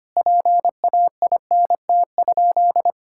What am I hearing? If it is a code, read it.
Morse code, heard as a keyed tone, about 25 wpm, 700 Hz. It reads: PAINT?
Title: Sentences from Top 1000 Words - Review Part 6 - 25wpm